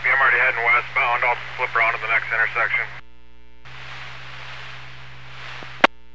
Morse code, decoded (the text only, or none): none